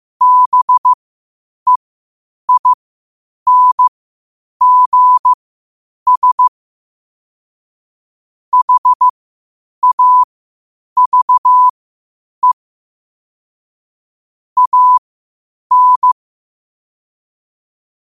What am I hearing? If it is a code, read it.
Morse code, heard as a keyed tone, about 15 wpm, spaced as for 8 wpm, 1000 Hz. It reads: BEINGS HAVE AN
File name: qotdfiles / 8wpm